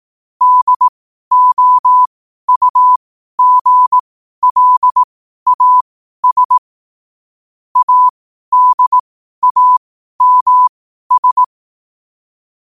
Morse code, heard as a keyed tone, about 18 wpm, spaced as for 12 wpm, 1000 Hz. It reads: DOUGLAS ADAMS
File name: qotdfiles / 12wpm